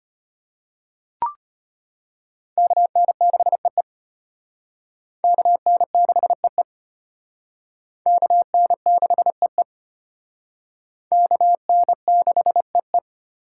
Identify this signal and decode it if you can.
Morse code, heard as a keyed tone, about 31 wpm, 700 Hz. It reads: KN6EE KN6EE KN6EE KN6EE
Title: Speed Racing - USA 2x2 Call Signs 25wpm